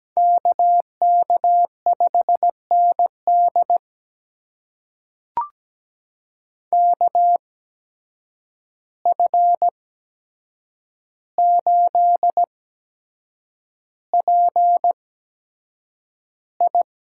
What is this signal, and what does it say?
Morse code, heard as a keyed tone, about 17 wpm, spaced as for 7 wpm, 700 Hz. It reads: KK5ND KF8PI
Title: USA Call Signs - 5 Characters - 8x Character Spacing - 17wpm